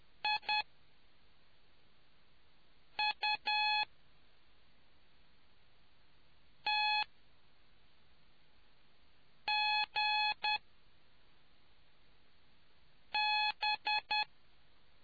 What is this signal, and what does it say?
Morse code, heard as a keyed tone, about 10 wpm, spaced as for 3 wpm, 850 Hz. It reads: IUTGB